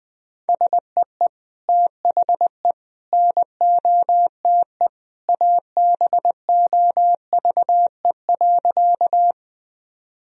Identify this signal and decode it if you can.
Morse code, heard as a keyed tone, about 20 wpm, 700 Hz. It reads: SEE THE NOTE ABOVE.
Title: Sentences from Top 1000 Words - Review Part 2 - 20wpm